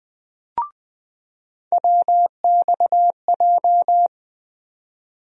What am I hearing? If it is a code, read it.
Morse code, heard as a keyed tone, about 21 wpm, 700 Hz. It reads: WXJ
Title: USA Call Sign Postfixes - 3 Letters 20wpm